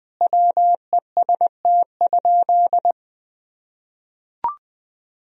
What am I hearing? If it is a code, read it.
Morse code, heard as a keyed tone, about 20 wpm, 700 Hz. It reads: WEST?
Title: Sentences from Top 1000 Words - Review Part 2 - 20wpm